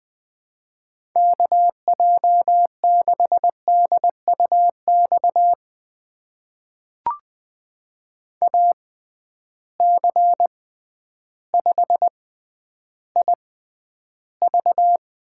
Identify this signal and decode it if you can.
Morse code, heard as a keyed tone, about 20 wpm, spaced as for 11 wpm, 700 Hz. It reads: KJ6DUX AC5IV
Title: USA Call Signs - 6x Character Spacing - 20wpm